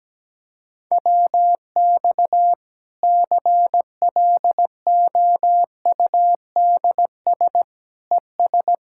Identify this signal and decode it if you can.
Morse code, heard as a keyed tone, about 17 wpm, 700 Hz. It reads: WX CLOUDS ES
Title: Practice QSO - Exchange 2 - Info - 17wpm